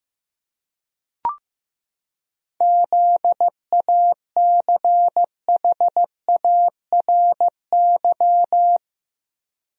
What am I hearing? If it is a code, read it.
Morse code, heard as a keyed tone, about 15 wpm, 700 Hz. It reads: ZACHARY
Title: USA Common Names - 15wpm